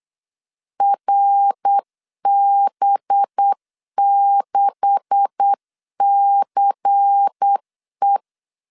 Morse code, heard as a keyed tone, about 8 wpm, 800 Hz. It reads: RB6CE
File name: LEZIONI / 9 286 8wpm 3425